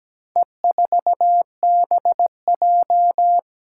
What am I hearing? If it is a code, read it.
Morse code, heard as a keyed tone, about 17 wpm, 700 Hz. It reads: E4BJ
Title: Four Challenging Letter-Number Combinations 17wpm